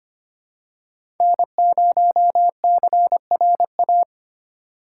Morse code, heard as a keyed tone, about 25 wpm, 700 Hz. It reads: N0CRA
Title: USA Call Signs - 5 Characters - 8x Character Spacing - 25wpm